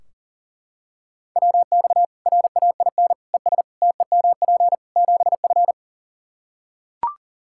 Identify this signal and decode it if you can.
Morse code, heard as a keyed tone, about 40 wpm, 700 Hz. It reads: WX RAIN ES TEMP 7F
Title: Practice QSO - Exchange 2 - Info - 40wpm